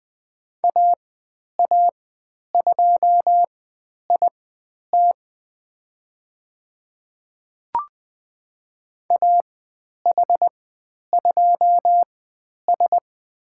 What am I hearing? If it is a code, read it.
Morse code, heard as a keyed tone, about 20 wpm, spaced as for 10 wpm, 700 Hz. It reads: AA2IT AH2S
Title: 095 Lesson 16g - Letter c - Call Signs 20wpm